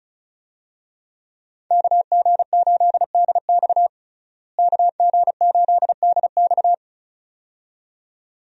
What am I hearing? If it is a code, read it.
Morse code, heard as a keyed tone, about 34 wpm, 700 Hz. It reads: KG8DX KG8DX
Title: USA Call Signs - 5 Characters - Repeated 2x - 35wpm